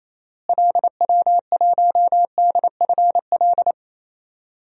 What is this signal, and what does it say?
Morse code, heard as a keyed tone, about 28 wpm, 700 Hz. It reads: LW1DFL
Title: DXCC-100 Call Signs - 6 Characters - Repeated 3x - 28wpm